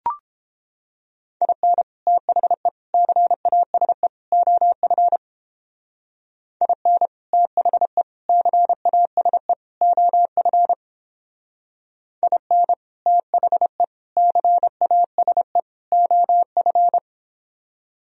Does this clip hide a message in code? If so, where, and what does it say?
Yes: Morse code from 1.4 to 17.0 s, heard as a keyed tone, about 29 wpm, 700 Hz. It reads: IN THE CASE OF IN THE CASE OF IN THE CASE OF